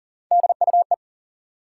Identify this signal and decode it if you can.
Morse code, heard as a keyed tone, about 40 wpm, 700 Hz. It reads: DUE
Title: Three Letter Words - 40wpm